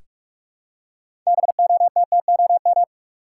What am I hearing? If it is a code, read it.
Morse code, heard as a keyed tone, about 42 wpm, 700 Hz. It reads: BOTTOM
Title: Six Letter Words - 45wpm